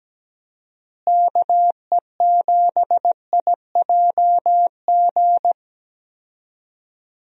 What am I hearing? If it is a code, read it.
Morse code, heard as a keyed tone, about 17 wpm, 700 Hz. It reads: KE7IJG
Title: USA Call Signs - 6 Characters - Repeated 2x - 17wpm